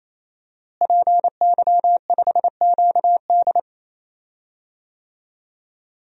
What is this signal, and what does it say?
Morse code, heard as a keyed tone, about 28 wpm, 700 Hz. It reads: PY5QD
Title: DXCC-100 Call Signs - 5 Characters - Repeated 3x - 28wpm